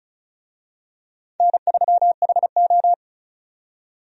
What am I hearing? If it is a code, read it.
Morse code, heard as a keyed tone, about 36 wpm, 700 Hz. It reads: N3HO
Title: USA Call Signs - 4 Characters - 35wpm